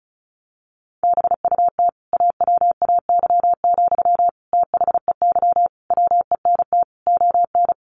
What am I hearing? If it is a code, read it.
Morse code, heard as a keyed tone, about 34 wpm, 700 Hz. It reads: BUT AWAY, THEY WENT ON
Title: Sentences from Top 1400 Words 35wpm